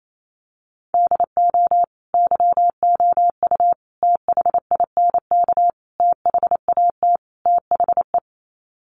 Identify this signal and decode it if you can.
Morse code, heard as a keyed tone, about 28 wpm, 690 Hz. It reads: DO YOU THINK THAT THE
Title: Sets of 5 Words Encore 28wpm